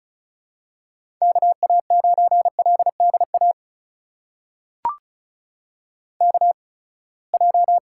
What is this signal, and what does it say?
Morse code, heard as a keyed tone, about 35 wpm, spaced as for 24 wpm, 700 Hz. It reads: KA9LDA KJ
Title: USA Call Signs - 6 Characters - 8x Character Spacing - 35wpm